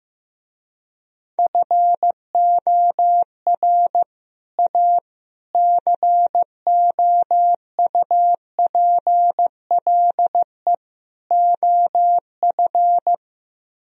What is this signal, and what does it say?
Morse code, heard as a keyed tone, about 15 wpm, 700 Hz. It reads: FOR A COUPLE OF
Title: Speed-Racing - Sets of 4 Words 15wpm